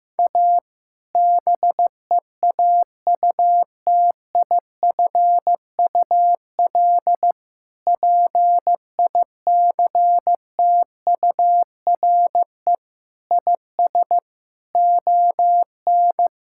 Morse code, heard as a keyed tone, about 16 wpm, 700 Hz. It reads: A BEAUTIFUL PICTURE IS ON
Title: Sentences from Top 1000 Words - Review Part 6 - 15wpm